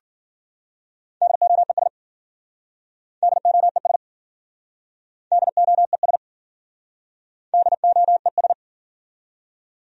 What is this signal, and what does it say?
Morse code, heard as a keyed tone, about 53 wpm, 700 Hz. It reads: DOES DOES DOES DOES